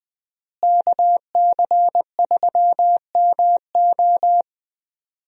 Morse code, heard as a keyed tone, about 20 wpm, 700 Hz. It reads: KC3MO
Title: USA Call Signs - 5 Characters - Repeated 3x - 20wpm